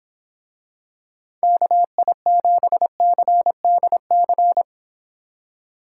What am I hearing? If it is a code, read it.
Morse code, heard as a keyed tone, about 26 wpm, 700 Hz. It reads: KI7CDC